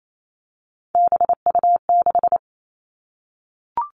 Morse code, heard as a keyed tone, about 28 wpm, 700 Hz. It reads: BU6
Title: Three Challenging Letter-Number Combinations 28wpm